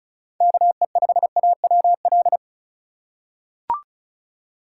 Morse code, heard as a keyed tone, about 34 wpm, 700 Hz. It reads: KE5AWL
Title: USA Call Signs - 6 Characters - 3x Character Spacing - 35wpm